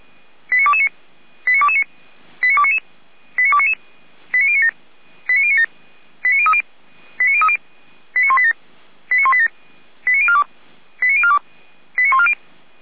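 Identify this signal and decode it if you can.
Five-tone selective call (ZVEI), heard as quick runs of five tones, tones of about 70 ms. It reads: 89229 89229 89299 89299 89097 89097 89030 89030 89187 89187 89042 89042 89140
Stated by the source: ZVEI five-tone selective call received by SDR, fire brigade (Feuerwehr) Orlamünde.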